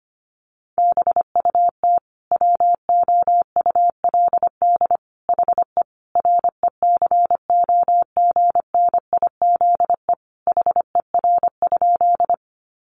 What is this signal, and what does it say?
Morse code, heard as a keyed tone, about 25 wpm, 700 Hz. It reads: BUT WOULD HE RECOGNIZE HER?
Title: Sentences from Top 1400 Words 25wpm